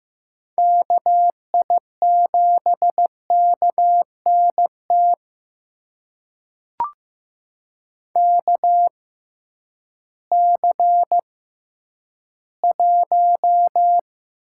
Morse code, heard as a keyed tone, about 15 wpm, spaced as for 10 wpm, 700 Hz. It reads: KI7KNT KC1
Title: USA Call Signs - 6 Characters - 6x Character Spacing - 15wpm